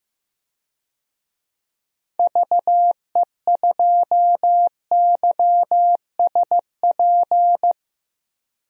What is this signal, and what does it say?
Morse code, heard as a keyed tone, about 15 wpm, 700 Hz. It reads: VE2YSP
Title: DXCC-100 Call Signs - 6 Characters - Repeated 2x - 15wpm